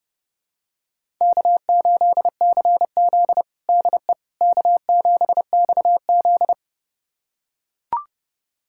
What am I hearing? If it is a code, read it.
Morse code, heard as a keyed tone, about 30 wpm, 700 Hz. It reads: K8CZ DE K7XZ
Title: USA Callsign DE Callsign - 4 Characters - 30wpm